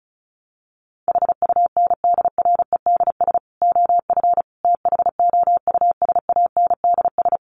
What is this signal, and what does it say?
Morse code, heard as a keyed tone, about 34 wpm, 700 Hz. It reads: HUNDREDS OF THOUSANDS
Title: Sets of 4 Words 35wpm